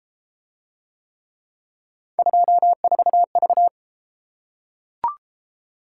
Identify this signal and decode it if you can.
Morse code, heard as a keyed tone, about 28 wpm, 700 Hz. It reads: 24V